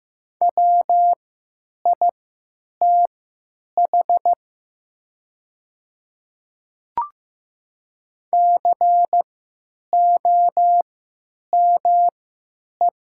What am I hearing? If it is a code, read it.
Morse code, heard as a keyed tone, about 15 wpm, spaced as for 9 wpm, 700 Hz. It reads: WITH COME
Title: 3x Character Spacing - Top 100 Words - 15wpm